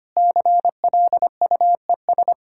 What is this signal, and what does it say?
Morse code, heard as a keyed tone, about 25 wpm, 700 Hz. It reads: CLUES
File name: practice / 25wpm / Sentences with 4-Word Phrases - 25wpm